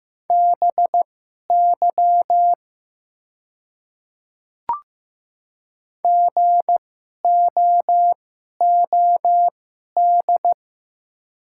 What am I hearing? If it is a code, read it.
Morse code, heard as a keyed tone, about 15 wpm, spaced as for 11 wpm, 690 Hz. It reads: BY GOOD
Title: 2x Character Spacing - Top 300 Words - 15wpm